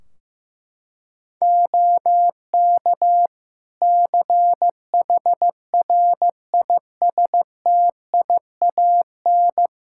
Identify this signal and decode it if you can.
Morse code, heard as a keyed tone, about 15 wpm, 700 Hz. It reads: OK CHRISTIAN
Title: Practice QSO - Exchange 2 - Info - 15wpm